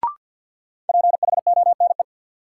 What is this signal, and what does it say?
Morse code, heard as a keyed tone, about 53 wpm, 700 Hz. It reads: PHONE
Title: Sentences from Top 1100 Words - New Word Drill - 50wpm